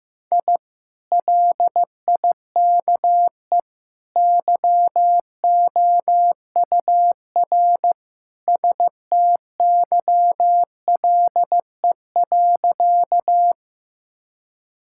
Morse code, heard as a keyed tone, about 16 wpm, 700 Hz. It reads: I LIKE YOUR STYLE.
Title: Sentences from Top 1000 Words - Review Part 5 - 15wpm